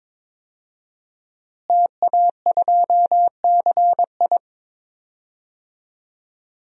Morse code, heard as a keyed tone, about 22 wpm, 700 Hz. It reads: TA2CI
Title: DXCC-100 Call Signs - 5 Characters - Repeated 2x - 22wpm